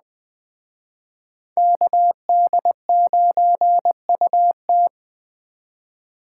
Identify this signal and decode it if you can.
Morse code, heard as a keyed tone, about 21 wpm, 700 Hz. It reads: KD9UT
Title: Speed Racing - USA 2x2 Call Signs 17wpm